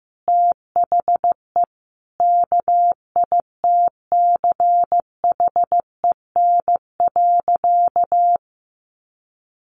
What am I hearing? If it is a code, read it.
Morse code, heard as a keyed tone, about 15 wpm, 700 Hz. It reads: THE KITCHEN.
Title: Sentences from Top 1400 Words 15wpm